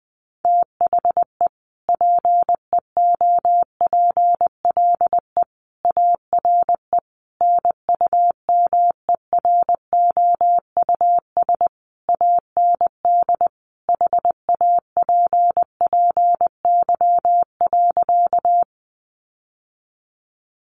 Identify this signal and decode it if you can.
Morse code, heard as a keyed tone, about 20 wpm, 700 Hz. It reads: THE PEOPLE ARE NUMEROUS AND HAPPY.
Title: Sentences from Top 1400 Words 20wpm